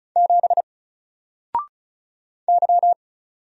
Morse code, heard as a keyed tone, about 36 wpm, 700 Hz. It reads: 7 Y
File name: Single-Letter-Number-35wpm